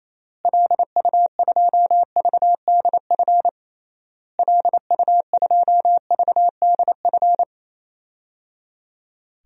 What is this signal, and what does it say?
Morse code, heard as a keyed tone, about 27 wpm, 700 Hz. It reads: LU2VDF LU2VDF
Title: DXCC-100 Call Signs - 6 Characters - Repeated 3x - 28wpm